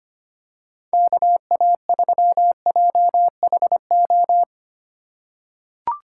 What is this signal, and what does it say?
Morse code, heard as a keyed tone, about 25 wpm, 690 Hz. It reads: KA3JHO
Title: USA Call Signs - 6 Characters - 2x Character Spacing - 25wpm